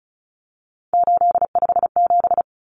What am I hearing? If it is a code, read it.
Morse code, heard as a keyed tone, about 34 wpm, 700 Hz. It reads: 857
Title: Common Cut Numbers - 3-Digit 35wpm